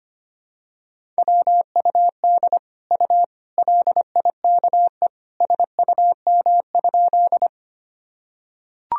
Morse code, heard as a keyed tone, about 25 wpm, 700 Hz. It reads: WUD U LIKE SUM?
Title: Sentences from Top 100 Words and Abbreviations 25wpm